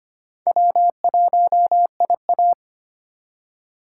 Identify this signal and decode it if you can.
Morse code, heard as a keyed tone, about 25 wpm, 700 Hz. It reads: W1IA
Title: USA Call Signs - 4 Characters - 3x Character Spacing - 25wpm